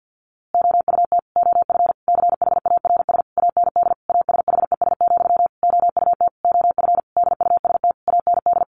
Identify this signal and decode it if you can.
Morse code, heard as a keyed tone, about 50 wpm, 700 Hz. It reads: OUT OF CHARS AND ASHES, OUT OF DUST AND
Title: A Sound Of Thunder 50wpm